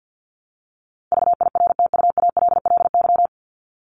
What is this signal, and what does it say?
Morse code, heard as a keyed tone, about 50 wpm, 700 Hz. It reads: VIRTUALLY
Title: Top 1501-1750 Words 50wpm